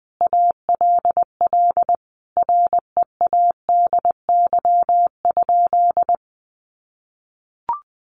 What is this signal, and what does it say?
Morse code, heard as a keyed tone, about 19 wpm, 700 Hz. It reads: ALL READY?
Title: Sentences from Top 400 Words 20wpm